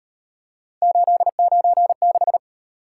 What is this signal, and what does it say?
Morse code, heard as a keyed tone, about 38 wpm, 700 Hz. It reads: 896